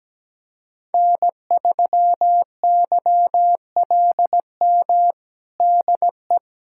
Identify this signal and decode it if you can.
Morse code, heard as a keyed tone, about 17 wpm, 700 Hz. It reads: N3YLM DE
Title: USA Callsign DE Callsign - 5 Characters - 17wpm